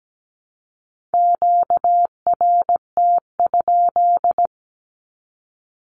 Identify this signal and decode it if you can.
Morse code, heard as a keyed tone, about 17 wpm, 700 Hz. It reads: QRT?
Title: Q-Codes Repeated 3x 17wpm